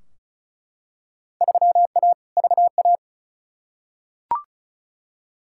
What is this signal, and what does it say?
Morse code, heard as a keyed tone, about 36 wpm, 700 Hz. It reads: 3A VA
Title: Field Day Exchanges - 35wpm